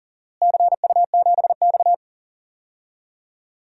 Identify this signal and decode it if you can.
Morse code, heard as a keyed tone, about 39 wpm, 700 Hz. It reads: CU7X
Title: DXCC-100 Call Signs - 4 Characters - Repeated 2x - 40wpm